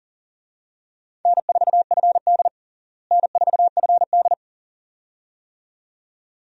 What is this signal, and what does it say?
Morse code, heard as a keyed tone, about 40 wpm, 700 Hz. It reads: N4FD N4FD
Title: USA Call Signs - 4 Characters - Repeated 2x - 40wpm